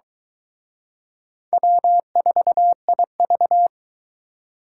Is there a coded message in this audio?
Yes: Morse code from 1.5 to 3.7 s, heard as a keyed tone, about 23 wpm, 700 Hz. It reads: W4IV